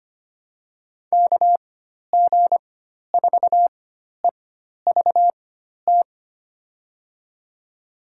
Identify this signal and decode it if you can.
Morse code, heard as a keyed tone, about 25 wpm, spaced as for 12 wpm, 700 Hz. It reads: KG4EVT
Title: USA Call Signs - 4x Character Spacing - 25wpm